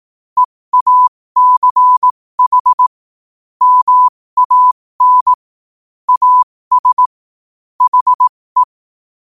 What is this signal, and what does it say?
Morse code, heard as a keyed tone, about 18 wpm, spaced as for 15 wpm, 1000 Hz. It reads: EACH MAN AS HE
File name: qotdfiles / 15wpm